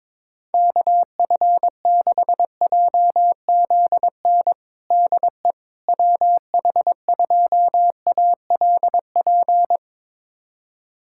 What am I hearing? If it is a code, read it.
Morse code, heard as a keyed tone, about 22 wpm, 700 Hz. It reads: KF6JZN DE WH2ALP
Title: USA Callsign DE Callsign - 6 Characters - 22wpm